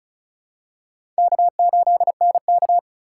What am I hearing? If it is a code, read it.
Morse code, heard as a keyed tone, about 36 wpm, 700 Hz. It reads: K8NK
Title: USA Call Signs - 4 Characters - 4x Character Spacing - 35wpm